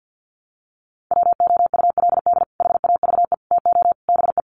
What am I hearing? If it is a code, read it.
Morse code, heard as a keyed tone, about 50 wpm, 700 Hz. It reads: WOULD HAVE TO BE
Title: Sets of 4 Words 50wpm